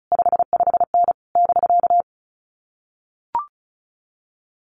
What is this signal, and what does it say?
Morse code, heard as a keyed tone, about 34 wpm, 700 Hz. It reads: 55N <BK>
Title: Common QSO Phrases Encore 35wpm